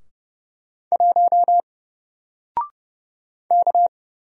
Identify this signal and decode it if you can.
Morse code, heard as a keyed tone, about 31 wpm, 700 Hz. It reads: 1 K
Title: Single Character (Letter:Number:Punctuation:Prosign) 30wpm